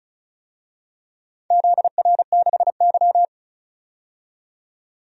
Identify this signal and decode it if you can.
Morse code, heard as a keyed tone, about 36 wpm, 700 Hz. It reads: ZR6Y
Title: DXCC-100 Call Signs - 4 Characters - 35wpm